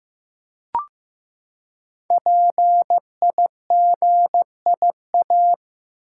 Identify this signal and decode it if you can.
Morse code, heard as a keyed tone, about 15 wpm, 700 Hz. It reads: PIGIA
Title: Code Groups - 5 Letters - 15wpm